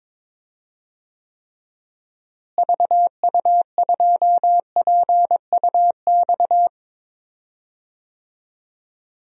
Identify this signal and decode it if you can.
Morse code, heard as a keyed tone, about 22 wpm, 700 Hz. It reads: VU2PUX
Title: DXCC-100 Call Signs - 6 Characters - Repeated 3x - 22wpm